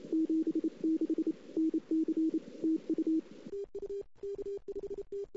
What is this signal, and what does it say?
Morse code, heard as a keyed tone, about 28 wpm, 320 Hz. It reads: 76 NC TU
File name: CW / 2008-ss-cw-k5tr-op-n5rz -00:00